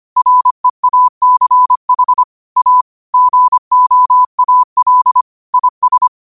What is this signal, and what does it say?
Morse code, heard as a keyed tone, about 25 wpm, 1000 Hz. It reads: REACH A GOAL IS
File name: qotdfiles / 25wpm